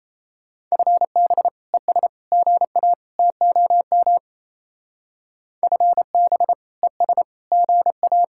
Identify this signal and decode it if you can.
Morse code, heard as a keyed tone, about 34 wpm, 700 Hz. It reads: FB ES GA TOM FB ES GA
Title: Speed-Racing Common QSO Phrases Encore - 28wpm